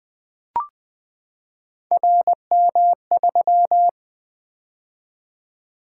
Morse code, heard as a keyed tone, about 20 wpm, 700 Hz. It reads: RM3
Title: Three Letter-Number Combinations - 20wpm